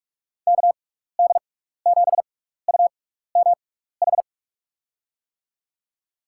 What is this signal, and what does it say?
Morse code, heard as a keyed tone, about 42 wpm, spaced as for 15 wpm, 700 Hz. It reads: KD7UMH
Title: USA Call Signs - 6 Characters - 6x Character Spacing - 45wpm